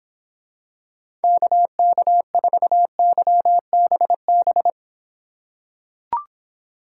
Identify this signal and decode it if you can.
Morse code, heard as a keyed tone, about 26 wpm, 700 Hz. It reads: KK4YBB